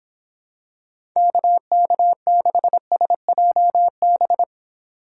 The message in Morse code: KK6SJB

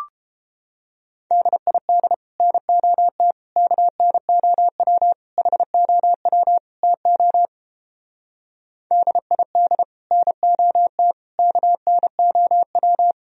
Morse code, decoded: DID NOT KNOW HOW TO DID NOT KNOW

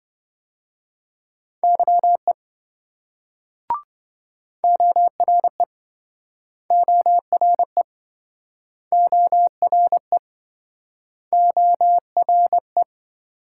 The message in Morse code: YE ORE ORE ORE ORE

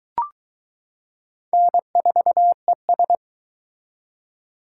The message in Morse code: N4ES

